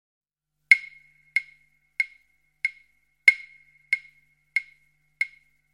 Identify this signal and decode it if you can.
There is no code.